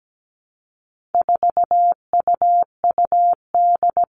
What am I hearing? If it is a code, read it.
Morse code, heard as a keyed tone, about 17 wpm, 700 Hz. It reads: 4UUD
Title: All Cut Numbers - 4-Digit 17wpm